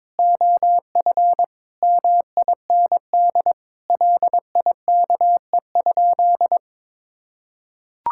Morse code, heard as a keyed tone, about 22 wpm, 700 Hz. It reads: OF MIND LIKE?
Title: Sentences from Top 1000 Words - Review Part 6 - 22wpm